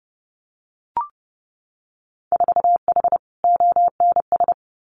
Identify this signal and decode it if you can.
Morse code, heard as a keyed tone, about 30 wpm, 700 Hz. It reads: 4H ONS
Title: Winter Field Day Exchanges Encore 30wpm